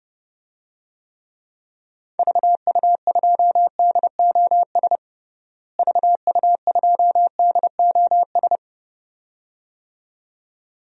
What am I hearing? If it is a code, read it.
Morse code, heard as a keyed tone, about 32 wpm, 700 Hz. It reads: VU2DOS VU2DOS
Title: DXCC-100 Call Signs - 6 Characters - Repeated 2x - 30wpm